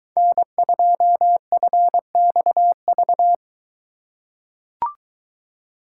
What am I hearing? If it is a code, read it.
Morse code, heard as a keyed tone, about 23 wpm, 700 Hz. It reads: N2FXV